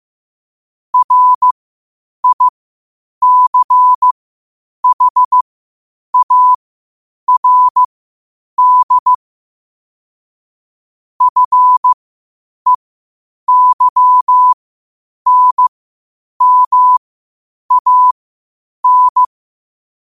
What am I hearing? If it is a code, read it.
Morse code, heard as a keyed tone, about 15 wpm, spaced as for 8 wpm, 1000 Hz. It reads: RICHARD FEYNMAN
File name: qotdfiles / 8wpm